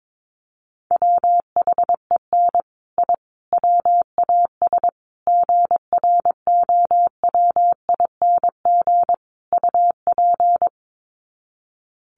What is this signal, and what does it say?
Morse code, heard as a keyed tone, about 22 wpm, 700 Hz. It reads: WHEN I WAS GROWING UP
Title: Sets of 5 Words Encore 22wpm